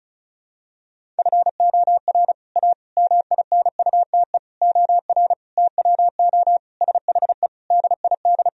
Morse code, heard as a keyed tone, about 34 wpm, 700 Hz. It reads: FOR A MINUTE OR TWO SHE DID
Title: Sentences from Top 1000 Words - Review Part 1 - 35wpm